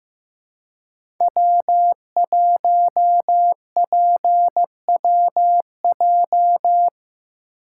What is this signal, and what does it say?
Morse code, heard as a keyed tone, about 15 wpm, 700 Hz. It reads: W1PWJ
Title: USA Call Signs - 5 Characters - 2x Character Spacing - 15wpm